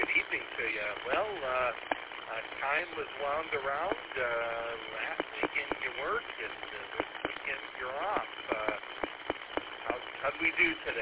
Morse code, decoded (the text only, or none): none